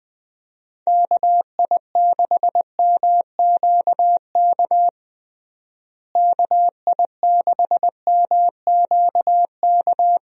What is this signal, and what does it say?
Morse code, heard as a keyed tone, about 20 wpm, 700 Hz. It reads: KI6MQK KI6MQK
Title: USA Call Signs - 6 Characters - Repeated 3x - 20wpm